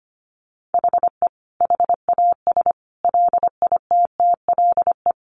Morse code, heard as a keyed tone, about 25 wpm, 700 Hz. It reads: HE HAS LITTLE
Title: Sentences from Top 1200 Words 25wpm